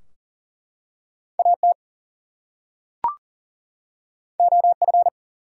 Morse code, heard as a keyed tone, about 38 wpm, 700 Hz. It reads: AT OF
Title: Two Letter Words - 40wpm